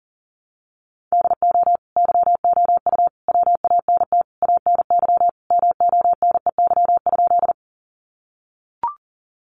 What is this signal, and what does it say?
Morse code, heard as a keyed tone, about 40 wpm, 700 Hz. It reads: DO YOU WANT ANY MONEY?
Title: Sentences from Top 400 Words 40wpm